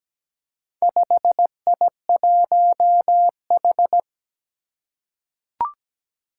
Morse code, heard as a keyed tone, about 17 wpm, 700 Hz. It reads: 5I1H
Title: Four Challenging Letter-Number Combinations 17wpm